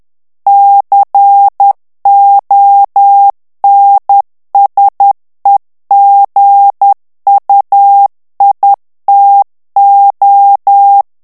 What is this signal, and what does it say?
Morse code, heard as a keyed tone, about 10 wpm, 800 Hz. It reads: CONSEGUITO
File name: ITALIEN texte clair 2 vitesse 10 mots minute